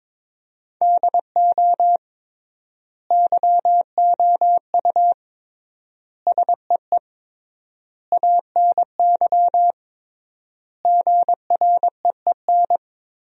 Morse code, heard as a keyed tone, about 22 wpm, 700 Hz. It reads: DO YOU SEE ANY GREEN
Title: 3x Word-Spacing - Sentences from Top 500 Words - 22wpm